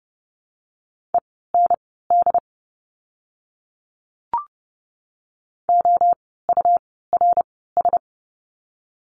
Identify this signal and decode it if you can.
Morse code, heard as a keyed tone, about 30 wpm, spaced as for 17 wpm, 700 Hz. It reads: END OURS
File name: Lesson 17f - Letter u - Words 30wpm